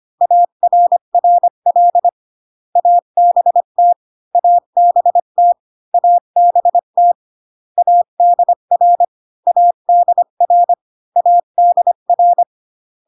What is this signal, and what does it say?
Morse code, heard as a keyed tone, about 25 wpm, 700 Hz. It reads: ARRL ABT ABT ABT ADR ADR ADR